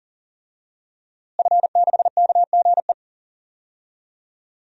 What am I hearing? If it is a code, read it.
Morse code, heard as a keyed tone, about 41 wpm, 700 Hz. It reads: F6KGE
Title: DXCC-100 Call Signs - 5 Characters - Repeated 2x - 40wpm